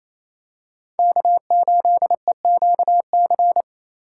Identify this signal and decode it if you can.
Morse code, heard as a keyed tone, about 28 wpm, 700 Hz. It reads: K8EQC
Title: USA Call Signs - 3x Character Spacing - 28wpm